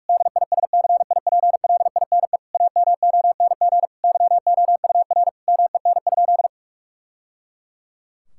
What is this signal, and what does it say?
Morse code, heard as a keyed tone, about 41 wpm, 700 Hz. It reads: DISCIPLINE AMONG YOUR MEN?
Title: Sentences from Top 1000 Words - Review Part 6 - 45wpm